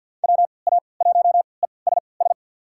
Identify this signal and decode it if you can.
Morse code, heard as a keyed tone, about 50 wpm, spaced as for 28 wpm, 700 Hz. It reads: WA1ESS
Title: USA Call Signs - 3x Character Spacing - 50wpm